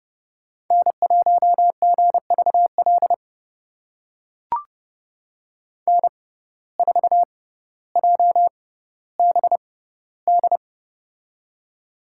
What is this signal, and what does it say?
Morse code, heard as a keyed tone, about 30 wpm, spaced as for 15 wpm, 700 Hz. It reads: N1GVL N4JBD
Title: USA Call Signs - 5 Characters - 6x Character Spacing - 30wpm